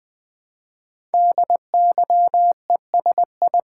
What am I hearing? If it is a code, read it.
Morse code, heard as a keyed tone, about 20 wpm, 700 Hz. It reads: DYESI
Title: Code Groups - 5 Letters - 20wpm